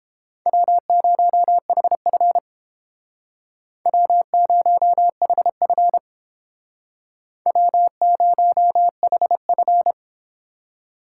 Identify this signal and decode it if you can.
Morse code, heard as a keyed tone, about 30 wpm, 700 Hz. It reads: W0HF W0HF W0HF